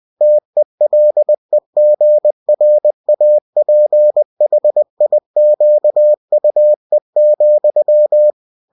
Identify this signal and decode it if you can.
Morse code, heard as a keyed tone, about 20 wpm, 600 Hz. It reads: TELEGRAPHIQUE,